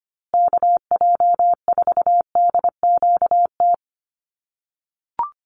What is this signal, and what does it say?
Morse code, heard as a keyed tone, about 25 wpm, 700 Hz. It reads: KJ4DQT
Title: USA Call Signs Repeated 3x 25wpm